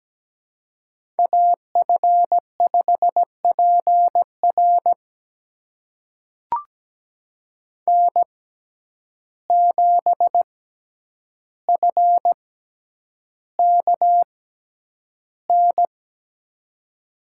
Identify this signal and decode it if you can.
Morse code, heard as a keyed tone, about 17 wpm, spaced as for 9 wpm, 700 Hz. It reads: AF5PR N7FKN
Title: USA Call Signs - 5 Characters - 6x Character Spacing - 17wpm